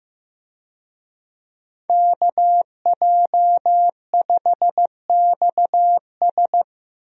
Morse code, heard as a keyed tone, about 15 wpm, 700 Hz. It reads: KJ5XS
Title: USA Call Signs - 5 Characters - Repeated 2x - 15wpm